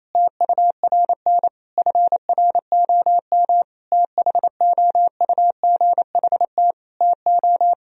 Morse code, heard as a keyed tone, about 28 wpm, 700 Hz. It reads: TURN FROM THOUGHT TO
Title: Sentences from Top 1000 Words - Review Part 4 - 28wpm